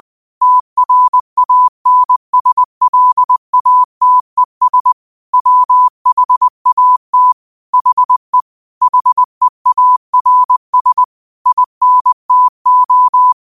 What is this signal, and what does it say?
Morse code, heard as a keyed tone, about 20 wpm, 1000 Hz. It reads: TRANSLATES WHAT HE HEARS INTO
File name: qotdfiles / 20wpm